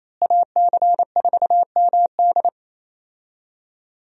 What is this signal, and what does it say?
Morse code, heard as a keyed tone, about 28 wpm, 700 Hz. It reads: AC4MD
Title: USA Call Signs - 5 Characters - Repeated 3x - 28wpm